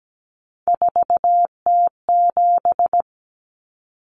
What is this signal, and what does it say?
Morse code, heard as a keyed tone, about 17 wpm, 700 Hz. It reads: 4T7
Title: Common Cut Numbers - 3-Digit 17wpm